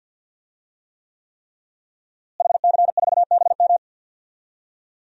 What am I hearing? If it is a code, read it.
Morse code, heard as a keyed tone, about 47 wpm, 700 Hz. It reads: HC4BM
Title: DXCC-100 Call Signs - 5 Characters - Repeated 3x - 50wpm